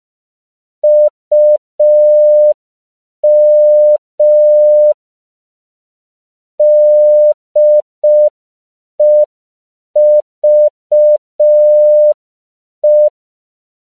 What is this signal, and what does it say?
Morse code, heard as a keyed tone, about 5 wpm, 600 Hz. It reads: UM DEVE
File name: 5PPM